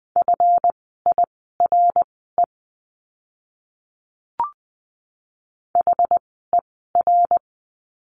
Morse code, heard as a keyed tone, about 20 wpm, spaced as for 14 wpm, 700 Hz. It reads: FIRE HER
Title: 139 Lesson 23f - Letter f - Words 20wpm